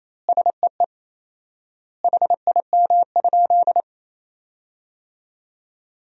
Morse code, heard as a keyed tone, about 28 wpm, 700 Hz. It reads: SEE HIM?
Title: 4x Word-Spacing - Sentences from Top 100 Words - 28wpm